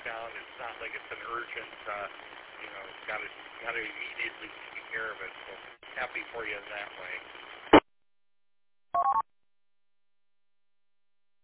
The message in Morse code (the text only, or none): none